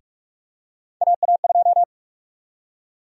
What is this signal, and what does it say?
Morse code, heard as a keyed tone, about 46 wpm, 700 Hz. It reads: AA2
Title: USA Call Sign Prefixes - 45wpm